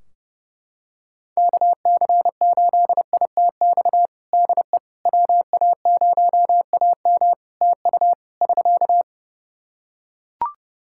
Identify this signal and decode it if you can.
Morse code, heard as a keyed tone, about 32 wpm, 700 Hz. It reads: KC8ITX DE WA0AM TU <SK>
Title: Practice QSO - Exchange 4 - Ending - 30wpm